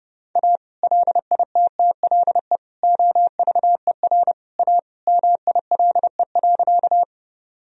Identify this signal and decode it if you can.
Morse code, heard as a keyed tone, about 30 wpm, 700 Hz. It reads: A LITTLE OVER A MILE.
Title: Sentences from Top 1000 Words - Review Part 1 - 30wpm